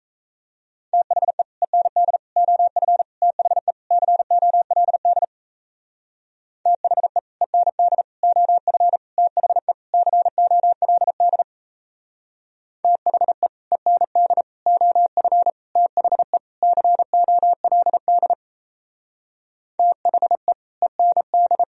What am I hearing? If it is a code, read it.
Morse code, heard as a keyed tone, about 33 wpm, 700 Hz. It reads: THE END OF THE COLD THE END OF THE COLD THE END OF THE COLD THE END
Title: Speed-Racing - Sets of 5 Words 28wpm